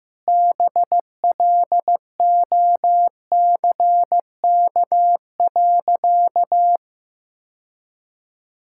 Morse code, heard as a keyed tone, about 16 wpm, 700 Hz. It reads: BLOCK.
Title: Sentences from Top 1600 Words 15wpm